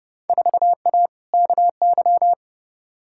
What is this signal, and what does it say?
Morse code, heard as a keyed tone, about 31 wpm, 700 Hz. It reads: 4A KY
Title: Field Day Exchanges Encore - 30wpm